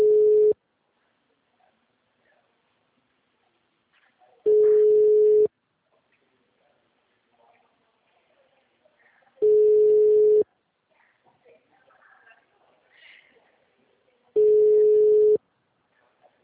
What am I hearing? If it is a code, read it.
Ringback tone (Europe, 425 Hz).